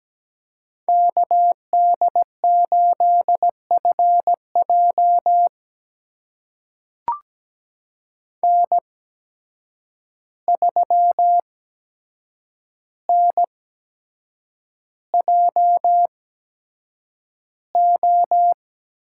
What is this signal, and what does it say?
Morse code, heard as a keyed tone, about 17 wpm, spaced as for 7 wpm, 700 Hz. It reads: KD8FJ N3NJO
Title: USA Call Signs - 5 Characters - 8x Character Spacing - 17wpm